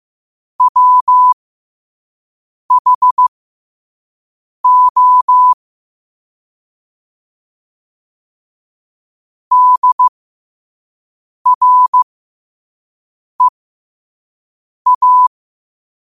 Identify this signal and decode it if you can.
Morse code, heard as a keyed tone, about 15 wpm, spaced as for 5 wpm, 1000 Hz. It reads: WHO DREA
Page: New quotes every day in morse code at 5 Words per minute.